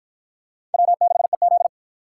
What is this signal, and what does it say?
Morse code, heard as a keyed tone, about 50 wpm, 700 Hz. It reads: W6EZ